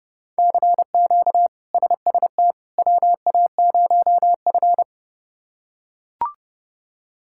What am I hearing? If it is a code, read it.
Morse code, heard as a keyed tone, about 30 wpm, 700 Hz. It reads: CQ SST WA0F
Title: K1USN SST QSO Practice - 30wpm